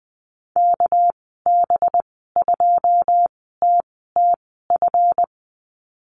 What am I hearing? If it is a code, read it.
Morse code, heard as a keyed tone, about 20 wpm, spaced as for 14 wpm, 700 Hz. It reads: KB2TTF
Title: 245 Lesson 39g - Letter z - Call Signs 20wpm